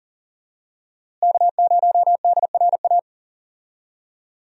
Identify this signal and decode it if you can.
Morse code, heard as a keyed tone, about 39 wpm, 700 Hz. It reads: K0DRA
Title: USA Call Signs - 5 Characters - Repeated 2x - 40wpm